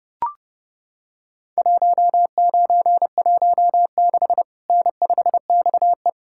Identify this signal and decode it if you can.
Morse code, heard as a keyed tone, about 30 wpm, 700 Hz. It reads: 1916 N5XE
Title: Worked All Europe DX Contest - QTC Exchange - 30wpm